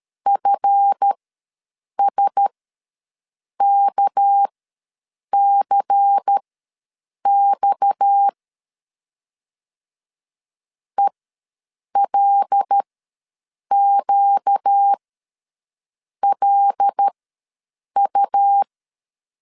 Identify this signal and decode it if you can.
Morse code, heard as a keyed tone, about 13 wpm, spaced as for 7 wpm, 800 Hz. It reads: FSKCX ELQLU
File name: LEZIONI / 8 YJ45 / 8 Y ws 12wpm 3409